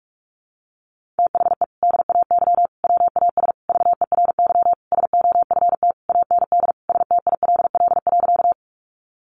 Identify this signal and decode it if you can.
Morse code, heard as a keyed tone, about 46 wpm, 700 Hz. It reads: THE DAY WAS VERY SOFT AND STILL.
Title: Sentences from Top 1500 Words 45wpm